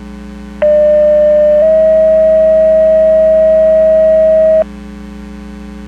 Two-tone page, heard as a one-second tone, then a three-second tone, about 600 then 640 Hz. It is heard from 0.6 to 4.6 s.
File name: Portage 154.385 Paging